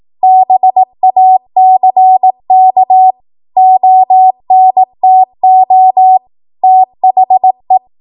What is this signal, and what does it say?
Morse code, heard as a keyed tone, about 18 wpm, 750 Hz. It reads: BACK ONTO THE
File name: Morse / 18WPM